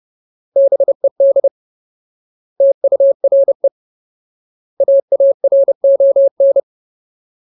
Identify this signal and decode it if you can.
Morse code, heard as a keyed tone, about 30 wpm, 550 Hz. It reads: BED TURE AARON
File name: Ultimate Word List / 30WPM Ultimate